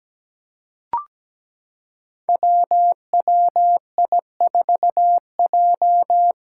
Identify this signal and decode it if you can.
Morse code, heard as a keyed tone, about 17 wpm, 700 Hz. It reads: WWI4J
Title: Five Challenging Letter-Number Combinations 17wpm